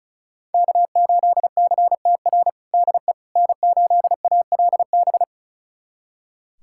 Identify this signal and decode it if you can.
Morse code, heard as a keyed tone, about 34 wpm, 700 Hz. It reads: K8CTR DE N8ALB
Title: USA Callsign DE Callsign - 5 Characters - 35wpm